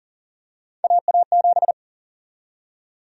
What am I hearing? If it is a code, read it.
Morse code, heard as a keyed tone, about 40 wpm, 700 Hz. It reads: AA7
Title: USA Call Sign Prefixes - Repeated 2x - 40wpm